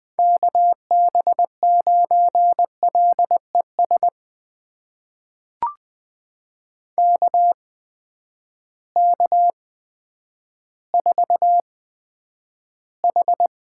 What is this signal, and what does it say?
Morse code, heard as a keyed tone, about 20 wpm, spaced as for 10 wpm, 700 Hz. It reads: KB9LES KK4H
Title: USA Call Signs - 6 Characters - 8x Character Spacing - 20wpm